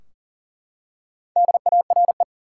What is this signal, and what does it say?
Morse code, heard as a keyed tone, about 40 wpm, 700 Hz. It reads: DARE
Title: Four Letter Words - 40wpm